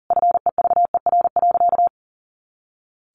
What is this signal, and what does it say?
Morse code, heard as a keyed tone, about 40 wpm, 700 Hz. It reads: FEVER.
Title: A Sound Of Thunder 40wpm